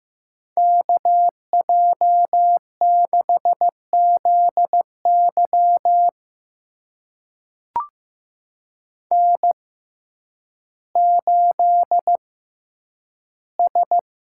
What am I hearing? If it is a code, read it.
Morse code, heard as a keyed tone, about 15 wpm, spaced as for 9 wpm, 700 Hz. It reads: KJ6ZY N8S
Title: USA Call Signs - 5 Characters - 6x Character Spacing - 15wpm